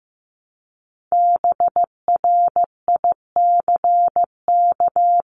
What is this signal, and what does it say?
Morse code, heard as a keyed tone, about 15 wpm, 700 Hz. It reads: BRICK
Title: Top 1751-2000 Words 15wpm